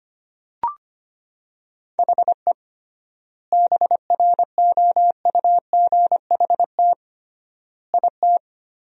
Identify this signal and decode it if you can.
Morse code, heard as a keyed tone, about 25 wpm, 700 Hz. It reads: HE BROUGHT IT
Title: 3x Word-Spacing - Sentences from Top 500 Words - 25wpm